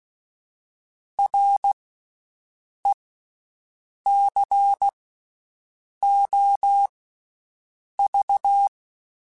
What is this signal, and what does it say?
Morse code, heard as a keyed tone, about 16 wpm, spaced as for 6 wpm, 800 Hz. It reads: RECOV
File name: Special INGs-mp3s / Special INGs-mp3s / Slow ing @15